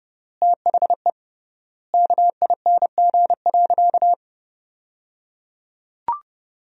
Morse code, heard as a keyed tone, about 30 wpm, 700 Hz. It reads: THE KING.
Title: 3x Word-Spacing - Sentences from Top 400 Words - 30wpm